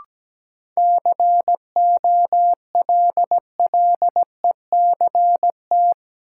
Morse code, heard as a keyed tone, about 17 wpm, 700 Hz. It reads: COLLECT
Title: Sentences from Top 1600 Words - New Word Drill - 17wpm